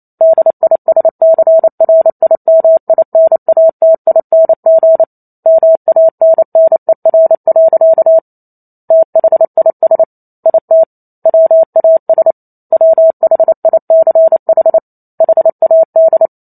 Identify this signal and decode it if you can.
Morse code, heard as a keyed tone, about 28 wpm, 650 Hz. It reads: DISCRIMINATING MANNER. THIS IT WAS WHICH HAD